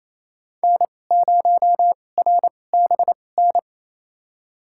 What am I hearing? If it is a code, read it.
Morse code, heard as a keyed tone, about 28 wpm, spaced as for 20 wpm, 700 Hz. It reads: N0RBN
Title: USA Call Signs - 5 Characters - 2x Character Spacing - 28wpm